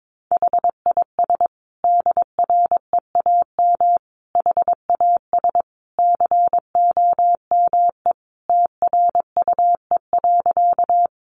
Morse code, heard as a keyed tone, about 22 wpm, 700 Hz. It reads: HIS DREAM HAS COME TRUE.
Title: Sentences from Top 1500 Words 22wpm